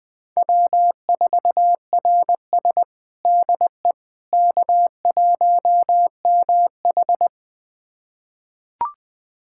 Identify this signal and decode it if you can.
Morse code, heard as a keyed tone, about 20 wpm, 700 Hz. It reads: W4RS DE K1MH
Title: USA Callsign DE Callsign - 4 Characters - 20wpm